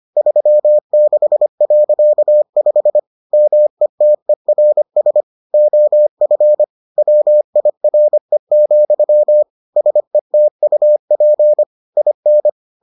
Morse code, heard as a keyed tone, about 25 wpm, 600 Hz. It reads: 36.5 METERS OF WIRE, SETUP IN